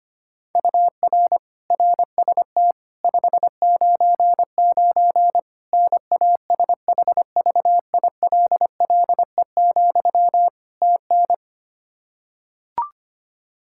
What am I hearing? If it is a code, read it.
Morse code, heard as a keyed tone, about 25 wpm, 700 Hz. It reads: UR RST 599 NASHVILLE, TN
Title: Practice QSO - Exchange 1 - Beginning - 25wpm